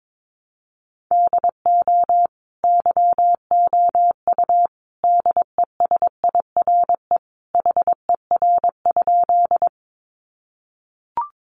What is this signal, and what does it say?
Morse code, heard as a keyed tone, about 22 wpm, 700 Hz. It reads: DO YOU DESIRE HER?
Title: Sentences from Top 1100 Words 22wpm